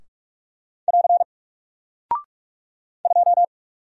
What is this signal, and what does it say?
Morse code, heard as a keyed tone, about 46 wpm, 700 Hz. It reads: <AR> 2
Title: Single Character (Letter:Number:Punctuation:Prosign) Alt 45wpm